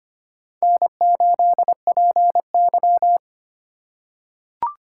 Morse code, heard as a keyed tone, about 25 wpm, 700 Hz. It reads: N8PY
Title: USA Call Signs - 4 Characters - 4x Character Spacing - 25wpm